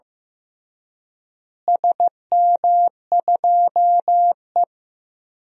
Morse code, heard as a keyed tone, about 15 wpm, 700 Hz. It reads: SM2E
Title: DXCC-100 Call Signs - 4 Characters - Repeated 2x - 15wpm